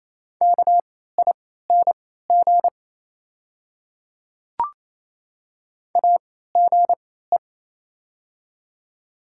Morse code, heard as a keyed tone, about 28 wpm, spaced as for 16 wpm, 700 Hz. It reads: KING AGE
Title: 3x Character Spacing - Top 500 Words - 28wpm